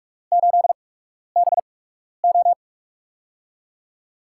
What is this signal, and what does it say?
Morse code, heard as a keyed tone, about 44 wpm, spaced as for 12 wpm, 700 Hz. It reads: 8BO
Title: USA Call Signs - 8x Character Spacing - 45wpm